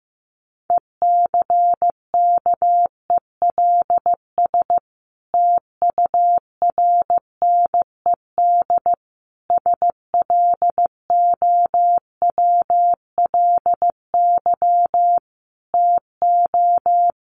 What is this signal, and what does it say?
Morse code, heard as a keyed tone, about 15 wpm, 700 Hz. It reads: ECKELS TURNED SLOWLY TO
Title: A Sound Of Thunder 15wpm